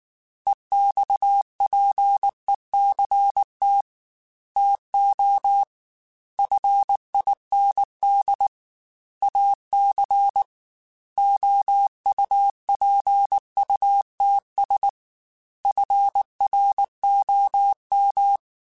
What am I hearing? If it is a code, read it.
Morse code, heard as a keyed tone, about 19 wpm, 800 Hz. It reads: EXPECT TO FIND AC OUPUTS FROM
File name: Technical Subjects / Technical Subjects-mp3s / Sw Pwr Sup @18